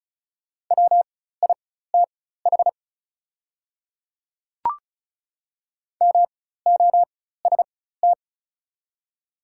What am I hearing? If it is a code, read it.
Morse code, heard as a keyed tone, about 36 wpm, spaced as for 16 wpm, 700 Hz. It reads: WITH MOST
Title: 4x Character Spacing - Top 100 Words - 35wpm